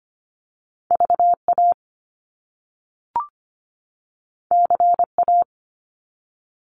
Morse code, heard as a keyed tone, about 25 wpm, 700 Hz. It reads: VA CA
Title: US States 25wpm